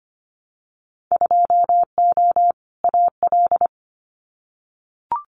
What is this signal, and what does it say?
Morse code, heard as a keyed tone, about 25 wpm, 700 Hz. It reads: 2O AL
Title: Winter Field Day Exchanges 25wpm